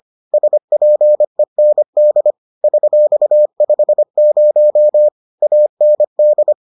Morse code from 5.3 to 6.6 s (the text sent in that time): AND